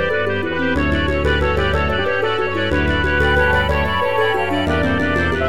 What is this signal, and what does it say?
There is no code.